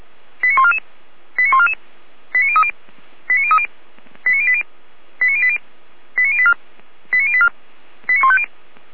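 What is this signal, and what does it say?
Five-tone selective call (ZVEI), heard as quick runs of five tones, tones of about 80 ms. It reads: 89144 89144 89030 89030 89080 89080 89084 89084 89150
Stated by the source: ZVEI five-tone selective call 89172, received by SDR.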